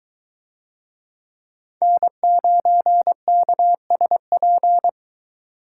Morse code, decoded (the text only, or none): N9KSP